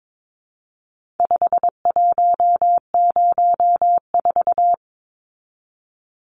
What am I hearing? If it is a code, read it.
Morse code, heard as a keyed tone, about 22 wpm, 700 Hz. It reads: 5104
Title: Numbers - 4-Digit 22wpm